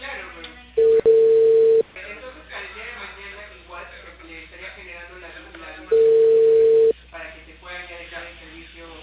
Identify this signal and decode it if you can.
Ringback tone (China, 450 Hz), with speech.